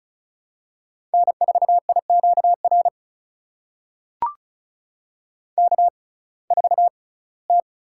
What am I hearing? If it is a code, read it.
Morse code, heard as a keyed tone, about 36 wpm, spaced as for 22 wpm, 700 Hz. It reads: N4IQR K4T
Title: USA Call Signs - 6x Character Spacing - 35wpm